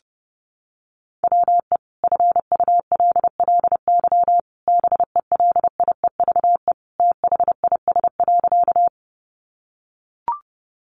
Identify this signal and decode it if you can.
Morse code, heard as a keyed tone, about 30 wpm, 700 Hz. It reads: WE FULLY BELIEVE THIS.
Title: Sentences from Top 1300 Words 30wpm